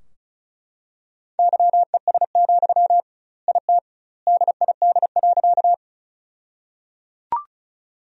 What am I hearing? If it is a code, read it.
Morse code, heard as a keyed tone, about 36 wpm, 700 Hz. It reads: YES, IT DID.
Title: 2x Word-Spacing - Sentences from Top 500 Words - 35wpm